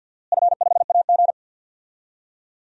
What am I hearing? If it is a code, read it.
Morse code, heard as a keyed tone, about 49 wpm, 700 Hz. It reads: F5AG
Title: DXCC-100 Call Signs - 4 Characters - Repeated 3x - 50wpm